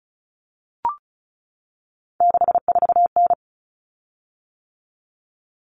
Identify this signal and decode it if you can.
Morse code, heard as a keyed tone, about 34 wpm, 700 Hz. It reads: 64N
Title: Common Cut Numbers - 3-Digit 35wpm